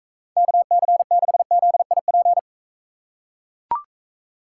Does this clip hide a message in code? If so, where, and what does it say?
Yes: Morse code from 0.4 to 2.4 s, heard as a keyed tone, about 40 wpm, 700 Hz. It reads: KC6ZIP